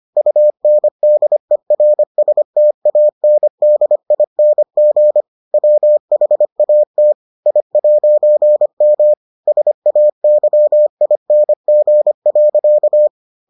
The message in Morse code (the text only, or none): UNDERSTANDING WHAT I'M SAYING.